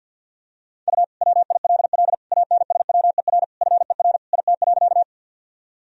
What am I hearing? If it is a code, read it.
Morse code, heard as a keyed tone, about 46 wpm, 700 Hz. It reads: U WILL ANSWER FER IT.
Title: Sentences from Top 300 Words and Abbreviations 50wpm